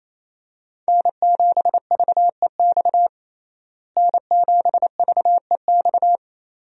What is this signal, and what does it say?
Morse code, heard as a keyed tone, about 28 wpm, 700 Hz. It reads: N7VEX N7VEX
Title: USA Call Signs - 5 Characters - Repeated 2x - 28wpm